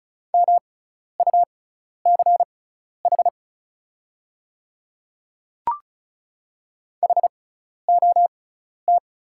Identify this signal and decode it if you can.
Morse code, heard as a keyed tone, about 34 wpm, spaced as for 12 wpm, 690 Hz. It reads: MUCH HOT
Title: 6x Character Spacing - Top 200 Words - 35wpm